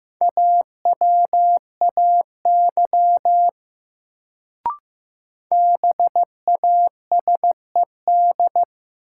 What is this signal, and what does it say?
Morse code, heard as a keyed tone, about 15 wpm, 700 Hz. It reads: AWAY BASED
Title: Sets of 2 Words - New Word Drill - 15wpm